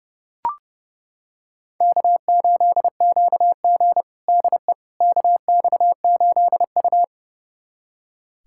Morse code, heard as a keyed tone, about 30 wpm, 700 Hz. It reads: K8QG DE KX8U
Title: USA Callsign DE Callsign - 4 Characters - 30wpm